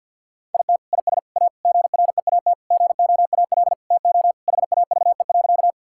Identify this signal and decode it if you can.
Morse code, heard as a keyed tone, about 50 wpm, 700 Hz. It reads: IT IS A GREAT GOAL TO HAVE.
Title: Sentences from Top 1000 Words - Review Part 5 - 50wpm